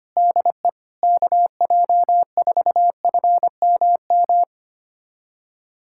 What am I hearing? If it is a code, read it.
Morse code, heard as a keyed tone, about 25 wpm, 700 Hz. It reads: DE KJ4FMM
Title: USA Callsign DE Callsign - 6 Characters - 25wpm